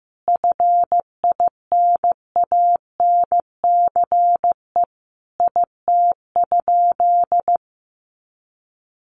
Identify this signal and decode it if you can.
Morse code, heard as a keyed tone, about 15 wpm, 700 Hz. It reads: FINANCE IT?
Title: Sentences from Top 1500 Words 15wpm